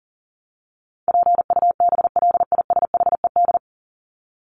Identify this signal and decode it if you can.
Morse code, heard as a keyed tone, about 40 wpm, 700 Hz. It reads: PUBLISHED
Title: ED Suffix - Top 100 Words with Base Words 40wpm